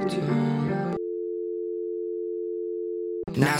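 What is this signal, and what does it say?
Dial tone (North America), 350 + 440 Hz, with speech.
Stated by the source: Dial tone used in a music track.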